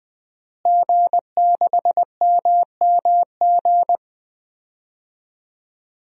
Morse code, heard as a keyed tone, about 20 wpm, 700 Hz. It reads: G6MMG
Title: DXCC-100 Call Signs - 5 Characters - Repeated 3x - 20wpm